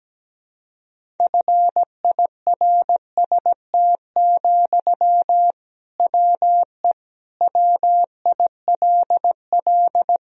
Morse code, heard as a keyed tone, about 17 wpm, 700 Hz. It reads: FIRST, WE WILL
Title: Sentences from Top 1000 Words - Review Part 1 - 17wpm